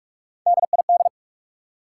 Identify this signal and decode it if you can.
Morse code, heard as a keyed tone, about 60 wpm, spaced as for 53 wpm, 700 Hz. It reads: DID
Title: Speed-Racing Top 180 QSO Elements 45wpm